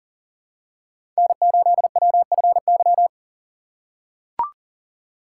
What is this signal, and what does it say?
Morse code, heard as a keyed tone, about 40 wpm, 700 Hz. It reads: N8WFY
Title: USA Call Signs - 5 Characters - 2x Character Spacing - 40wpm